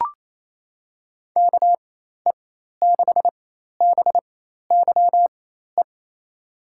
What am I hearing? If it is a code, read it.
Morse code, heard as a keyed tone, about 28 wpm, spaced as for 13 wpm, 700 Hz. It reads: KE6BYE
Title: USA Call Signs - 6 Characters - 4x Character Spacing - 28wpm